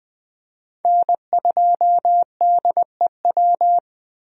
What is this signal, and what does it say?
Morse code, heard as a keyed tone, about 20 wpm, 700 Hz. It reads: N2DEW
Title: USA Call Signs - 5 Characters - Repeated 3x - 20wpm